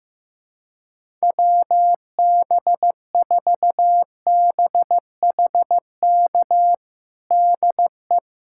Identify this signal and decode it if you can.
Morse code, heard as a keyed tone, about 15 wpm, 700 Hz. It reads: WB4BHK DE
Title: USA Callsign DE Callsign - 6 Characters - 15wpm